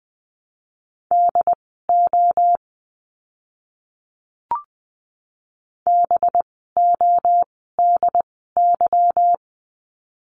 Morse code, heard as a keyed tone, about 20 wpm, spaced as for 15 wpm, 700 Hz. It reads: DO BODY
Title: 185 Lesson 30f - Letter b - Words 20wpm